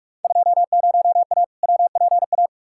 Morse code, heard as a keyed tone, about 47 wpm, 700 Hz. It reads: 20A WPA
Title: Field Day Exchanges Encore - 45wpm